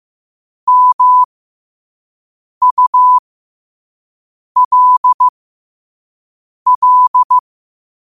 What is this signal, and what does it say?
Morse code, heard as a keyed tone, about 15 wpm, spaced as for 5 wpm, 1000 Hz. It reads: MULL
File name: qotdfiles / 5wpm